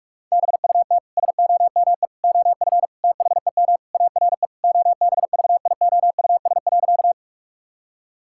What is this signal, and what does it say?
Morse code, heard as a keyed tone, about 46 wpm, 700 Hz. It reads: BUT SOME OF THEM ARE OBVIOUS.
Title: Sentences from Top 1600 Words 45wpm